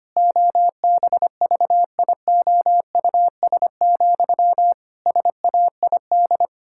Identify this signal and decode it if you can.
Morse code, heard as a keyed tone, about 25 wpm, 700 Hz. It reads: OBVIOUS, SAID
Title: Sentences from Top 1600 Words 25wpm